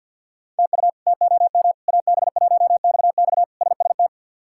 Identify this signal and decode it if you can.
Morse code, heard as a keyed tone, about 46 wpm, 700 Hz. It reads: TU TOM AB1XX SST
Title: K1USN SST QSO Practice - 50wpm